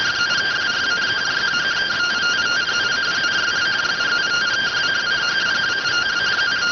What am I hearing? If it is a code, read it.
Two-tone teleprinter signal, about 170 Hz shift.